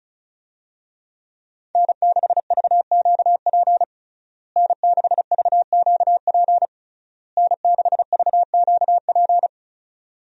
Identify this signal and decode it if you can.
Morse code, heard as a keyed tone, about 36 wpm, 700 Hz. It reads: N6VQP N6VQP N6VQP
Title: USA Call Signs - 5 Characters - Repeated 3x - 35wpm